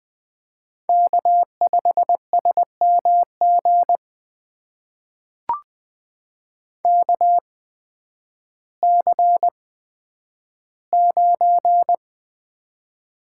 Morse code, heard as a keyed tone, about 20 wpm, spaced as for 11 wpm, 700 Hz. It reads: K5SMG KC9
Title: USA Call Signs - 5 Characters - 8x Character Spacing - 20wpm